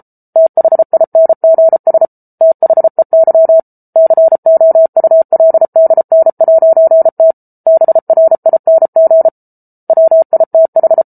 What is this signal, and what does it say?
Morse code, heard as a keyed tone, about 34 wpm, 650 Hz. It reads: THINGS THEY COULDN'T BRING WITH